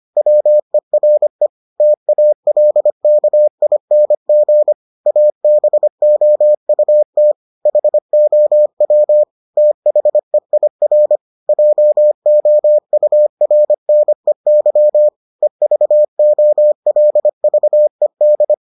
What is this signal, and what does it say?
Morse code, heard as a keyed tone, about 25 wpm, 600 Hz. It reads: WERE TALKING ABOUT HOW THEIR JOURNEY EVOLVED